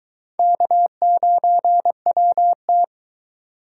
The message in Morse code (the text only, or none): K9WT